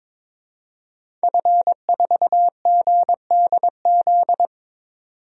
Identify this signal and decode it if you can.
Morse code, heard as a keyed tone, about 21 wpm, 700 Hz. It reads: F4GDZ
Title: DXCC-100 Call Signs - 5 Characters - 22wpm